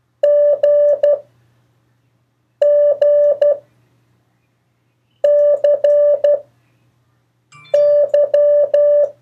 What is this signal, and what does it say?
Morse code, heard as a keyed tone, about 12 wpm, 580 Hz. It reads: G G C Y
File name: 1of4-cw-letters